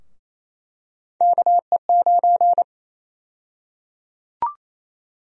Morse code, heard as a keyed tone, about 27 wpm, 700 Hz. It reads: KE9
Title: USA Call Sign Prefixes - 28wpm